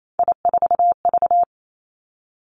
Morse code, heard as a keyed tone, about 28 wpm, 700 Hz. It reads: I4V
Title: Three Challenging Letter-Number Combinations 28wpm